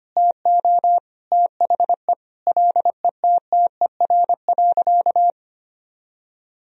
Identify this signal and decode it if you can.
Morse code, heard as a keyed tone, about 25 wpm, 700 Hz. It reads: TO THE LETTER.
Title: Sentences from Top 1000 Words - Review Part 1 - 25wpm